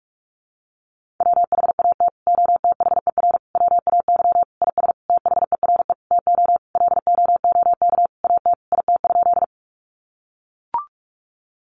Morse code, heard as a keyed tone, about 44 wpm, 700 Hz. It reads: WHAT OTHER WAY IS THERE TO LOOK AT IT?
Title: Sentences from Top 100 Words 45wpm